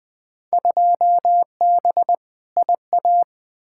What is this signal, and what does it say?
Morse code, heard as a keyed tone, about 20 wpm, 700 Hz. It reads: 2B IA
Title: Field Day Exchanges Encore - 20wpm